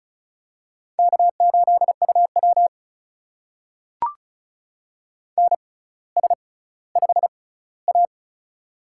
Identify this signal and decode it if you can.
Morse code, heard as a keyed tone, about 36 wpm, spaced as for 18 wpm, 700 Hz. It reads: K8UW NS5A
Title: USA Call Signs - 4 Characters - 6x Character Spacing - 35wpm